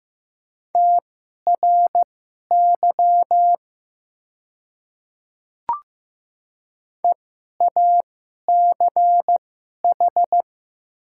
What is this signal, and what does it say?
Morse code, heard as a keyed tone, about 15 wpm, spaced as for 11 wpm, 700 Hz. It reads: TRY EACH
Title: 2x Character Spacing - Top 200 Words - 15wpm